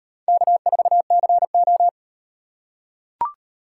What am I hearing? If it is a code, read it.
Morse code, heard as a keyed tone, about 36 wpm, 700 Hz. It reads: K4CO